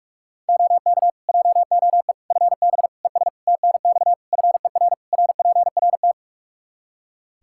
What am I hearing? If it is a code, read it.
Morse code, heard as a keyed tone, about 47 wpm, 700 Hz. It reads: OK JOE FB ES TNX FER RPRT
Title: K1USN SST QSO Practice 45wpm